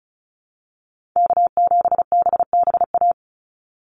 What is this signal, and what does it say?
Morse code, heard as a keyed tone, about 34 wpm, 700 Hz. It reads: K7BBA
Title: USA Call Signs 35wpm